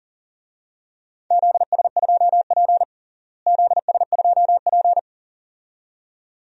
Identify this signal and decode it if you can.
Morse code, heard as a keyed tone, about 40 wpm, 700 Hz. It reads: ZS2P ZS2P
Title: DXCC-100 Call Signs - 4 Characters - Repeated 2x - 40wpm